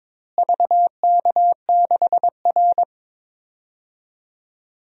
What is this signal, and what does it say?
Morse code, heard as a keyed tone, about 22 wpm, 700 Hz. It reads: VK6R
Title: DXCC-100 Call Signs - 4 Characters - Repeated 2x - 22wpm